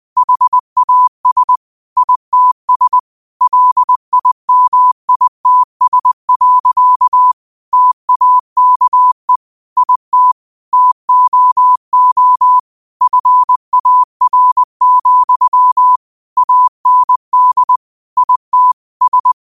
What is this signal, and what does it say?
Morse code, heard as a keyed tone, about 20 wpm, 1000 Hz. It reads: HAS ITS LIMITS. TAKE IT TOO FAR, AND IT S
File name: qotdfiles / 20wpm